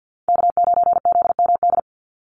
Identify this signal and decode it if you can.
Morse code, heard as a keyed tone, about 52 wpm, 700 Hz. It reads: K9ZMD
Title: NA SOTA Chaser Callsigns 50wpm